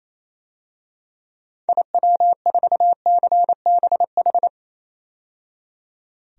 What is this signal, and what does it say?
Morse code, heard as a keyed tone, about 28 wpm, 700 Hz. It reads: IW4CBH
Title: DXCC-100 Call Signs - 6 Characters - 28wpm